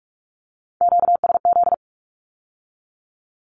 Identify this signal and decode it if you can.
Morse code, heard as a keyed tone, about 46 wpm, 700 Hz. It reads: QSZ
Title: Q-Codes with Spoken Meaning - 45wpm